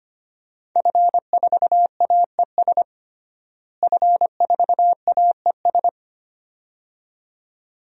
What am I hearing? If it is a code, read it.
Morse code, heard as a keyed tone, about 25 wpm, 700 Hz. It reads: F4AES F4AES
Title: DXCC-100 Call Signs - 5 Characters - Repeated 2x - 25wpm